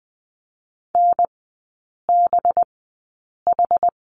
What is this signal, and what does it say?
Morse code, heard as a keyed tone, about 20 wpm, spaced as for 8 wpm, 700 Hz. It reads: NBH
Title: 184 Lesson 30e - Letter b - Three Characters 20wpm